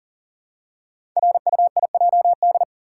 Morse code, heard as a keyed tone, about 43 wpm, 700 Hz. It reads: RUIJD
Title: Code Groups - 5 Letters - 40wpm